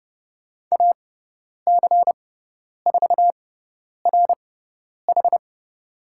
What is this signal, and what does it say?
Morse code, heard as a keyed tone, about 30 wpm, spaced as for 10 wpm, 700 Hz. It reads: AC4RH
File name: Lesson 17g - Letter u - Call Signs 30wpm